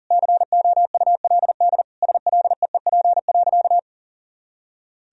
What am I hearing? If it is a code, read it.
Morse code, heard as a keyed tone, about 40 wpm, 700 Hz. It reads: COULD SLEEP.
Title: Sentences from Top 500 Words and Abbreviations 40wpm